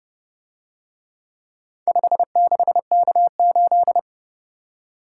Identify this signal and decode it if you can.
Morse code, heard as a keyed tone, about 29 wpm, 700 Hz. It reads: 56K8